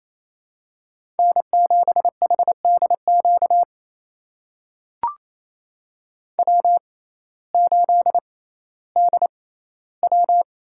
Morse code, heard as a keyed tone, about 28 wpm, spaced as for 15 wpm, 700 Hz. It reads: N7HDQ W8DW
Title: USA Call Signs - 5 Characters - 6x Character Spacing - 28wpm